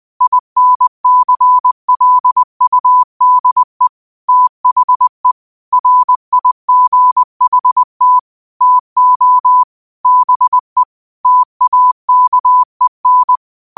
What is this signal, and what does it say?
Morse code, heard as a keyed tone, about 20 wpm, 1000 Hz. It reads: INCLUDE THE RIGHT TO BE TAKEN
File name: qotdfiles / 20wpm